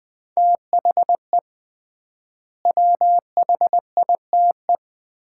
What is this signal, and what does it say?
Morse code, heard as a keyed tone, about 20 wpm, 700 Hz. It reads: THE WHITE
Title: 3x Word-Spacing - Sets of 3 Words - 20wpm